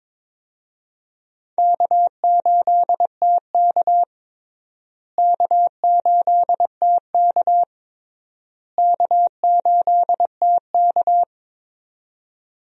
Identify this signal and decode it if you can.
Morse code, heard as a keyed tone, about 21 wpm, 700 Hz. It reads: K8TK K8TK K8TK
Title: USA Call Signs - 4 Characters - Repeated 3x - 22wpm